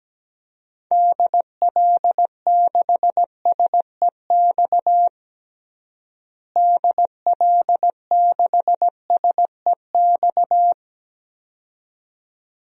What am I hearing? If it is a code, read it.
Morse code, heard as a keyed tone, about 17 wpm, 700 Hz. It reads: DL6SEX DL6SEX
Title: DXCC-100 Call Signs - 6 Characters - Repeated 2x - 17wpm